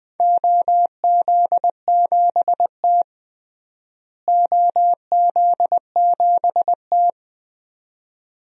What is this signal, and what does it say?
Morse code, heard as a keyed tone, about 21 wpm, 700 Hz. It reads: OZ7T OZ7T
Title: DXCC-100 Call Signs - 4 Characters - Repeated 2x - 20wpm